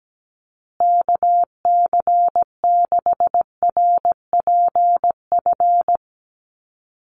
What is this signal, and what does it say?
Morse code, heard as a keyed tone, about 17 wpm, 700 Hz. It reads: KC6RPF
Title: USA Call Signs Repeated 3x 17wpm